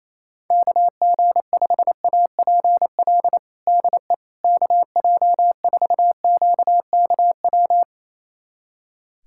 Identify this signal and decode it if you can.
Morse code, heard as a keyed tone, about 28 wpm, 700 Hz. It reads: KG5APL DE KJ4QKW
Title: USA Callsign DE Callsign 28wpm